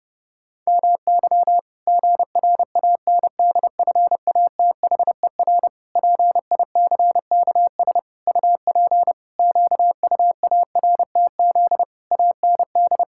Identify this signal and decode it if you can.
Morse code, heard as a keyed tone, about 30 wpm, 700 Hz. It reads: MY GRANDFATHER PICKS UP QUARTZ AND
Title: Pangram Challenge v2 - 30wpm